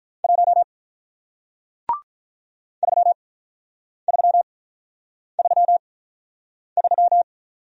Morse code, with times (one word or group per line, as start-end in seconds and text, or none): 0.2-0.6 1
2.8-7.2 3333